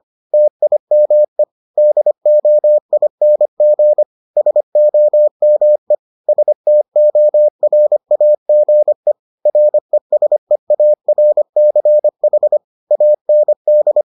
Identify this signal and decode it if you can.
Morse code, heard as a keyed tone, about 25 wpm, 600 Hz. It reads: TIME DOING SOME STORAGE RESEARCH AND